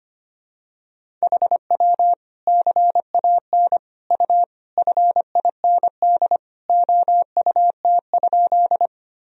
Morse code, heard as a keyed tone, about 25 wpm, 700 Hz. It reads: HW CAN U FIND OUT?
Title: Sentences from Top 200 Words and Abbreviations 25wpm